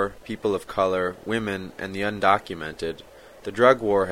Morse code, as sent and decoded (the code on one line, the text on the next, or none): none